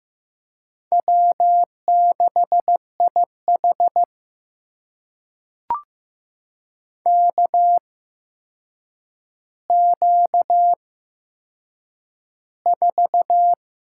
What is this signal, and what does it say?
Morse code, heard as a keyed tone, about 15 wpm, spaced as for 7 wpm, 700 Hz. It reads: W6IH KQ4
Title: USA Call Signs - 4 Characters - 8x Character Spacing - 15wpm